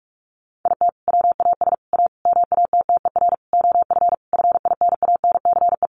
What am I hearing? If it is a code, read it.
Morse code, heard as a keyed tone, about 44 wpm, 700 Hz. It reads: IT WAS A MATTER OF FINANCE
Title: Sentences from Top 1500 Words 45wpm